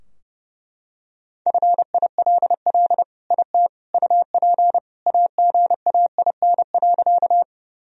Morse code, heard as a keyed tone, about 30 wpm, 700 Hz. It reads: FILL IT UP AGAIN.
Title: Sentences from Top 1000 Words - Review Part 3 - 30wpm